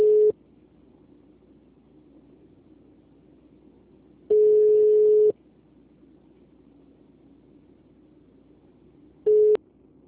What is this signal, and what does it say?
Ringback tone (Europe, 425 Hz).